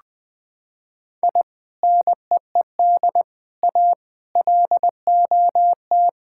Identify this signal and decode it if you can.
Morse code, heard as a keyed tone, about 20 wpm, 700 Hz. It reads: I NEED A LOT
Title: Sentences with 3-Word Phrases - 20wpm